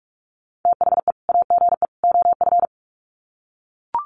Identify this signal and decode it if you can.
Morse code, heard as a keyed tone, about 44 wpm, 700 Hz. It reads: THE AGE OF
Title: Sets of 3 Words 45wpm